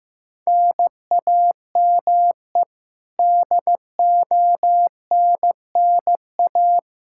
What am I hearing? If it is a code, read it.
Morse code, heard as a keyed tone, about 15 wpm, 700 Hz. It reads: NAME DONNA
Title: Practice QSO - Exchange 1 - Beginning - 15wpm